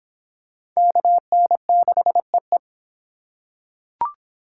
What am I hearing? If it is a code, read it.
Morse code, heard as a keyed tone, about 22 wpm, 700 Hz. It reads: KN6EE